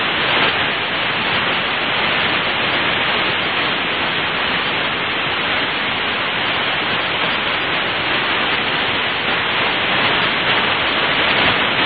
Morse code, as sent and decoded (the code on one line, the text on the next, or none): none